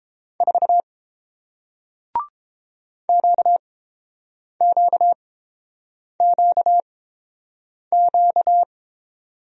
Morse code, heard as a keyed tone, about 30 wpm, spaced as for 7 wpm, 700 Hz. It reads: 4 QQQQ